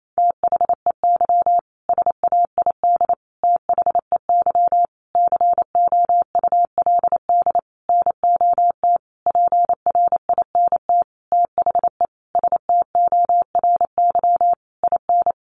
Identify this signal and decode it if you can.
Morse code, heard as a keyed tone, about 28 wpm, 700 Hz. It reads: THEY SAID THEY COULD NOT PRINT THE STORY IN
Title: Sentences from Top 1400 Words 28wpm